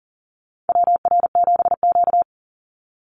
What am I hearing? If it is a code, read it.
Morse code, heard as a keyed tone, about 40 wpm, 700 Hz. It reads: WR7Q
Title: Call Signs - 40wpm